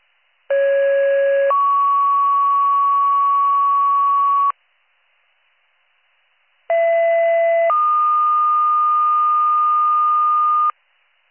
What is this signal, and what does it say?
Two-tone page, heard as a one-second tone, then a three-second tone, about 550 then 1120 Hz, about 670 then 1220 Hz. It is heard from 0.5 to 10.7 s.